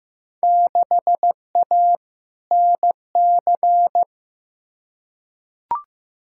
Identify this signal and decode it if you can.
Morse code, heard as a keyed tone, about 15 wpm, 700 Hz. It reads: 6A NC
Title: Field Day Exchanges Encore - 15wpm